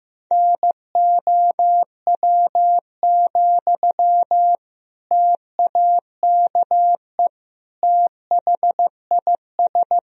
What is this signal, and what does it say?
Morse code, heard as a keyed tone, about 16 wpm, 700 Hz. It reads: NOW, TAKE THIS
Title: Sentences from Top 1000 Words - Review Part 2 - 15wpm